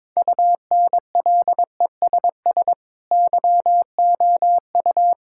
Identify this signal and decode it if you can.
Morse code, heard as a keyed tone, about 22 wpm, 700 Hz. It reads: UNLESS YOU
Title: Sentences from Top 1000 Words - Review Part 6 - 22wpm